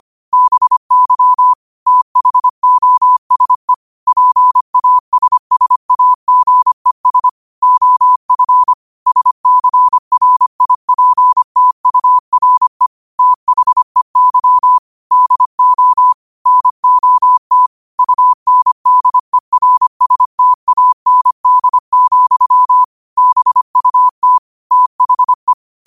Morse code, heard as a keyed tone, about 25 wpm, 1000 Hz. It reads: BY THOSE PASSAGES OF SCRIPTURE THEY DO NOT UNDERSTAND, BUT THE